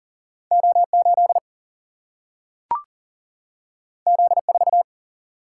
Morse code, heard as a keyed tone, about 40 wpm, 700 Hz. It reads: O8 Z4
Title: Single-Letter Single-Number 40wpm